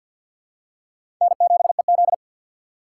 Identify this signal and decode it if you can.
Morse code, heard as a keyed tone, about 53 wpm, 700 Hz. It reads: N7EZ
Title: USA Call Signs - 4 Characters - 2x Character Spacing - 50wpm